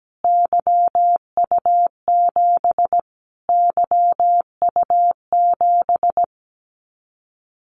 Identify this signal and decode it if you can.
Morse code, heard as a keyed tone, about 17 wpm, 700 Hz. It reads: YU7 YU7
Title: Callsign Prefixes for Top 25 DXCC - Repeated 3x - 17wpm